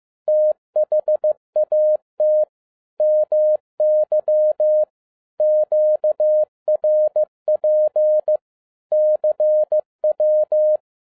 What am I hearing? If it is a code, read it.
Morse code, heard as a keyed tone, about 15 wpm, 610 Hz. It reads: THAT MY QRP CW
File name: 2011-07-mp3 / n42 EXCITING QRP CW OPERATION 15wpm